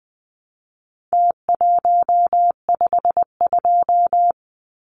Morse code, heard as a keyed tone, about 20 wpm, 700 Hz. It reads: T152
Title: Numbers - Contest Style - 4-Digit 20wpm